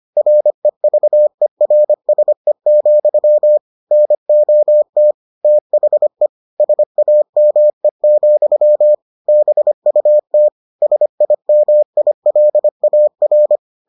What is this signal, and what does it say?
Morse code, heard as a keyed tone, about 25 wpm, 600 Hz. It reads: REVERSE, NOT THE SAME, BUT SIMILAR